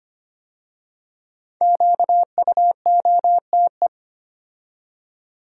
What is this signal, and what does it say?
Morse code, heard as a keyed tone, about 25 wpm, 700 Hz. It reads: QUOTE
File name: 5 Letter Words - First 3 Letters Given - 25wpm